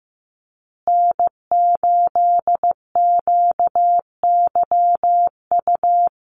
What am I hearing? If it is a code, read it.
Morse code, heard as a keyed tone, about 15 wpm, 700 Hz. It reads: N8QYU
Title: USA Call Signs Repeated 2x 15wpm